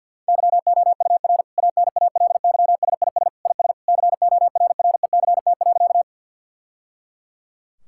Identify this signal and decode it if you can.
Morse code, heard as a keyed tone, about 50 wpm, 700 Hz. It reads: YOUR ANALYSIS IS CORRECT.
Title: Sentences from Top 1000 Words - Review Part 5 - 50wpm